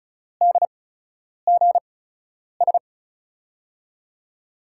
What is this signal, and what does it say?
Morse code, heard as a keyed tone, about 36 wpm, spaced as for 10 wpm, 700 Hz. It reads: DGS
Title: USA Call Signs - 8x Character Spacing - 35wpm